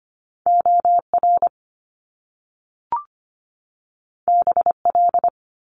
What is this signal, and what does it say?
Morse code, heard as a keyed tone, about 25 wpm, 700 Hz. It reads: OR BL
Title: Top Two Letter Combinations in Words 25wpm